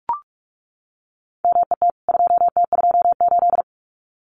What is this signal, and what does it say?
Morse code, heard as a keyed tone, about 44 wpm, 700 Hz. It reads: MET 2T28
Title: CWT Exchange 45wpm